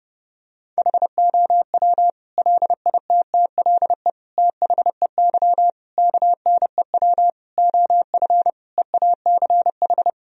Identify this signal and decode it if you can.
Morse code, heard as a keyed tone, about 30 wpm, 700 Hz. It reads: HOW LITTLE THEY KNEW OF EACH
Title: Sentences from Top 1000 Words - Review Part 1 - 30wpm